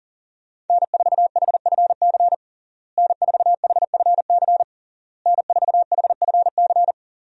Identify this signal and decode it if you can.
Morse code, heard as a keyed tone, about 40 wpm, 690 Hz. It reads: N4HFC N4HFC N4HFC
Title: USA Call Signs - 5 Characters - Repeated 3x - 40wpm